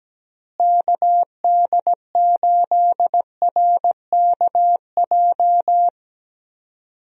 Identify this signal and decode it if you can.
Morse code, heard as a keyed tone, about 17 wpm, 700 Hz. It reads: KD8RKJ
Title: USA Call Signs - 6 Characters - Repeated 3x - 17wpm